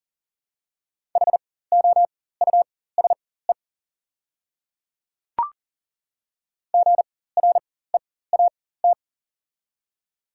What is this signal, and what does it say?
Morse code, heard as a keyed tone, about 40 wpm, spaced as for 19 wpm, 690 Hz. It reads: HOUSE GREAT
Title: 4x Character Spacing - Top 200 Words - 40wpm